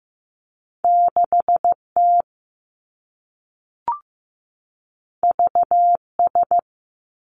Morse code, heard as a keyed tone, about 15 wpm, 700 Hz. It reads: 6 T V S
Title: Two Challenging Letter-Number Combinations 15wpm